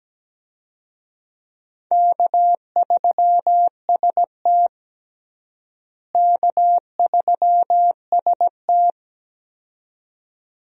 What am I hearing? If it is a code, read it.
Morse code, heard as a keyed tone, about 17 wpm, 700 Hz. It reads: K3ST K3ST
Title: USA Call Signs - 4 Characters - Repeated 2x - 17wpm